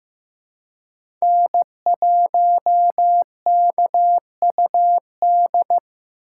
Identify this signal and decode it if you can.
Morse code, heard as a keyed tone, about 15 wpm, 700 Hz. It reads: N1KUD
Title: USA Call Signs - 5 Characters - 4x Character Spacing - 15wpm